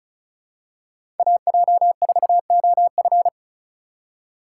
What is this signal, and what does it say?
Morse code, heard as a keyed tone, about 36 wpm, 700 Hz. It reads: AJ4OF
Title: USA Call Signs - 5 Characters - 2x Character Spacing - 35wpm